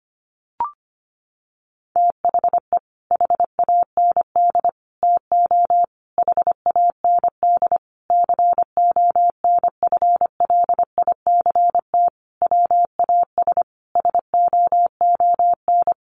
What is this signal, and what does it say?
Morse code, heard as a keyed tone, about 25 wpm, 700 Hz. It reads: THE HAND TO HAND CONFLICT WAS SOON
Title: Sentences from Top 1300 Words 25wpm